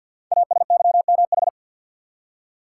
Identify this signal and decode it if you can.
Morse code, heard as a keyed tone, about 50 wpm, 690 Hz. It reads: ASYMH
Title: Code Groups - 5 Letters - 50wpm